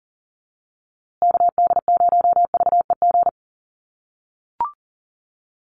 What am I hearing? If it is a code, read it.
Morse code, heard as a keyed tone, about 40 wpm, 700 Hz. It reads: KD0VEG
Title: USA Call Signs Repeated 2x 40wpm